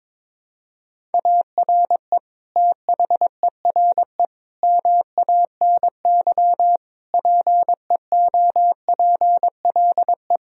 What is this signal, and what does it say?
Morse code, heard as a keyed tone, about 22 wpm, 700 Hz. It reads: ARE THERE MANY PEOPLE
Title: Sentences from Top 1000 Words - Review Part 1 - 22wpm